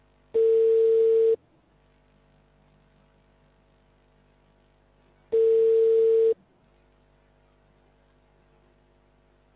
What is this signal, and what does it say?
Ringback tone (China, 450 Hz).